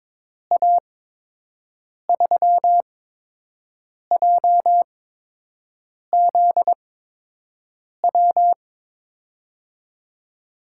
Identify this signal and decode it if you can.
Morse code, heard as a keyed tone, about 22 wpm, spaced as for 6 wpm, 700 Hz. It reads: A3JZW
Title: USA Call Signs - 6 Characters - 8x Character Spacing - 22wpm